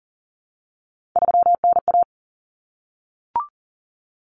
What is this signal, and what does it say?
Morse code, heard as a keyed tone, about 40 wpm, 700 Hz. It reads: 3NA
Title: Common Cut Numbers - 3-Digit 40wpm